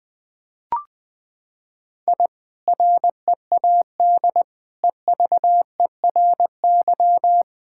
Morse code, heard as a keyed tone, about 21 wpm, 700 Hz. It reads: I READ EVERY
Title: Sentences from Top 1000 Words - Review Part 5 - 20wpm